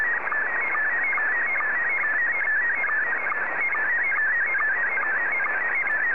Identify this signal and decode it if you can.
SSTV (slow-scan television), Scottie 1.